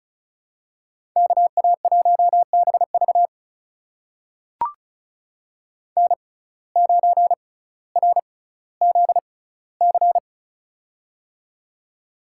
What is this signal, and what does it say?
Morse code, heard as a keyed tone, about 36 wpm, spaced as for 18 wpm, 700 Hz. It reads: KA1BV N9RZC
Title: USA Call Signs - 5 Characters - 6x Character Spacing - 35wpm